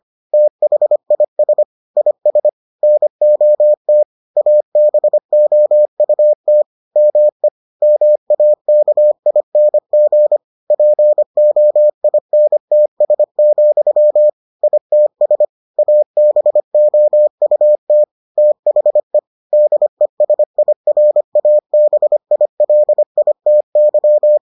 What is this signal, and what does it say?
Morse code, heard as a keyed tone, about 25 wpm, 600 Hz. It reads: THIS IS NOT ABOUT ME MAKING POINTS, ITS ABOUT THE DESIRABILITY